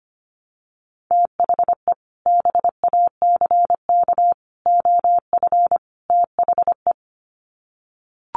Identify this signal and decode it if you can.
Morse code, heard as a keyed tone, about 25 wpm, 700 Hz. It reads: THE BACK OF THE
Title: Sets of 4 Words 25wpm